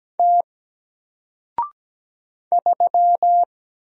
Morse code, heard as a keyed tone, about 17 wpm, 700 Hz. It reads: T 3
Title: Single Character (Letter:Number:Punctuation:Prosign) Alt 17wpm